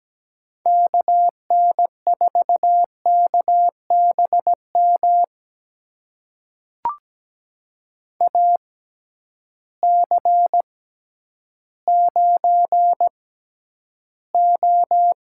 Morse code, heard as a keyed tone, about 17 wpm, spaced as for 10 wpm, 700 Hz. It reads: KN4KBM AC9O
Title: USA Call Signs - 6x Character Spacing - 17wpm